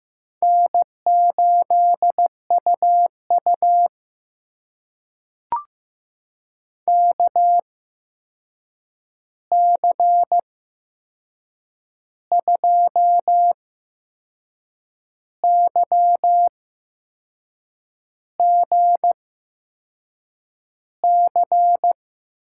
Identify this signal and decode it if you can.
Morse code, heard as a keyed tone, about 15 wpm, spaced as for 6 wpm, 700 Hz. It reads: N8UU KC2YGC
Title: USA Call Signs - 8x Character Spacing - 15wpm